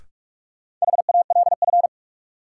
Morse code, heard as a keyed tone, about 46 wpm, 700 Hz. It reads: HALF
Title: Sets of 3 Words - New Word Drill - 45wpm